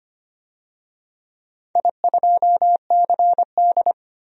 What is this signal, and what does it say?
Morse code, heard as a keyed tone, about 25 wpm, 700 Hz. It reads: I2CD
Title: DXCC-100 Call Signs - 4 Characters - 25wpm